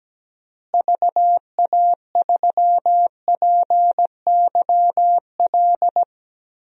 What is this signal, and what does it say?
Morse code, heard as a keyed tone, about 17 wpm, 700 Hz. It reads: VA3PYL
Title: DXCC-100 Call Signs - 6 Characters - Repeated 3x - 17wpm